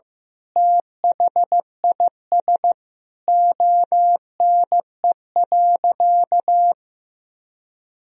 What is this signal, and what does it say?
Morse code, heard as a keyed tone, about 15 wpm, 700 Hz. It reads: THIS ONE.
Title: Sentences from Top 1000 Words - Review Part 4 - 15wpm